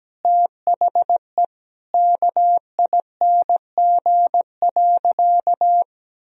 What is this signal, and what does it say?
Morse code, heard as a keyed tone, about 17 wpm, 700 Hz. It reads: THE KING.
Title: Sentences from Top 1000 Words - Review Part 2 - 17wpm